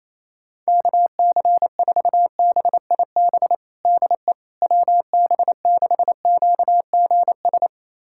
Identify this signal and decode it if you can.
Morse code, heard as a keyed tone, about 28 wpm, 700 Hz. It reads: KC4BIB DE WB6QGS
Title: USA Callsign DE Callsign - 6 Characters - 28wpm